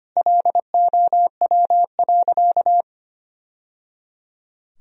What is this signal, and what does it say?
Morse code, heard as a keyed tone, about 25 wpm, 700 Hz. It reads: LOW.
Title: Sentences from Top 1000 Words - Review Part 1 - 25wpm